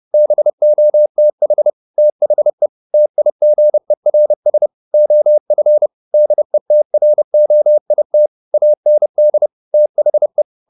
Morse code, heard as a keyed tone, about 30 wpm, 600 Hz. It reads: BOTH THE TIGERS OF DETROIT AND THE
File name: the old man and the sea / the old man and the sea 1 30wpm